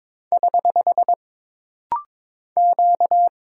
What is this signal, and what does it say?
Morse code, heard as a keyed tone, about 22 wpm, 700 Hz. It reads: <HH> Q
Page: This rapid-fire timing is geared to push your ICR (Instant Character Recognition) skills.